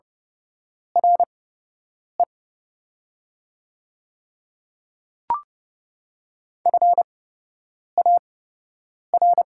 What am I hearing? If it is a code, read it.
Morse code, heard as a keyed tone, about 32 wpm, spaced as for 8 wpm, 700 Hz. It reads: RE FAR
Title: 8x Character Spacing - Top 300 Words - 30wpm